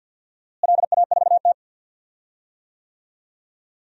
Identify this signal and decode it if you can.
Morse code, heard as a keyed tone, about 53 wpm, 700 Hz. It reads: LA4T
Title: DXCC-100 Call Signs - 4 Characters - Repeated 3x - 50wpm